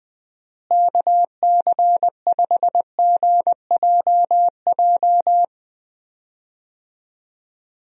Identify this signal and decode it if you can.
Morse code, heard as a keyed tone, about 21 wpm, 700 Hz. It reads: KC5GJJ
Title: USA Call Signs - 6 Characters - Repeated 2x - 20wpm